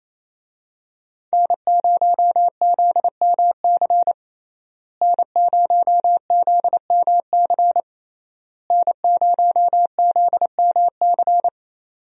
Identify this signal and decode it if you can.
Morse code, heard as a keyed tone, about 28 wpm, 700 Hz. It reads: N0ZMC N0ZMC N0ZMC
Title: USA Call Signs - 5 Characters - Repeated 3x - 28wpm